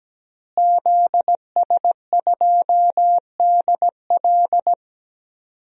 Morse code, heard as a keyed tone, about 17 wpm, 700 Hz. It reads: ZS2DL
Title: DXCC-100 Call Signs - 5 Characters - Repeated 2x - 17wpm